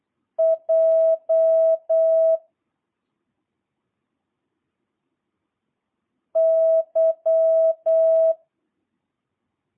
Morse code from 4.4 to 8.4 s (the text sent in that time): Y